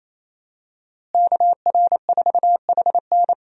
Morse code, heard as a keyed tone, about 27 wpm, 700 Hz. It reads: KR4HN
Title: USA Call Signs - 5 Characters - Repeated 3x - 28wpm